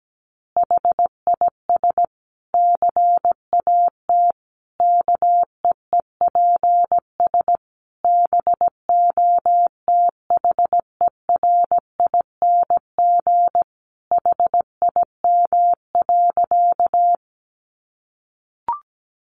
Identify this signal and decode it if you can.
Morse code, heard as a keyed tone, about 17 wpm, 700 Hz. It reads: HIS CAT KEEPS BOTHERING HIM.
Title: Additional Easy Phrases 17wpm